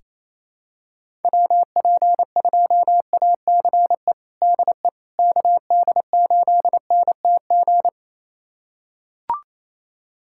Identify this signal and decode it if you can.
Morse code, heard as a keyed tone, about 27 wpm, 700 Hz. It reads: WP2ACE DE KD8NTG
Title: USA Callsign DE Callsign - 6 Characters - 28wpm